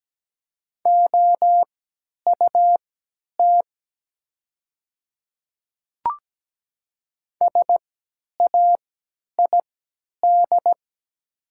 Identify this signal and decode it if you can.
Morse code, heard as a keyed tone, about 17 wpm, spaced as for 10 wpm, 700 Hz. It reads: OUT SAID
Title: 3x Character Spacing - Top 100 Words - 17wpm